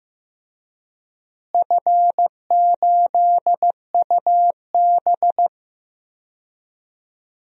Morse code, heard as a keyed tone, about 15 wpm, 700 Hz. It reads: F8UB
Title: DXCC-100 Call Signs - 4 Characters - Repeated 2x - 15wpm